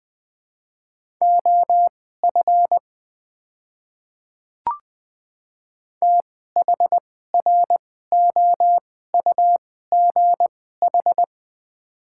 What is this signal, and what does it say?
Morse code, heard as a keyed tone, about 20 wpm, spaced as for 14 wpm, 700 Hz. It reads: OF THROUGH
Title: 2x Character Spacing - Top 200 Words - 20wpm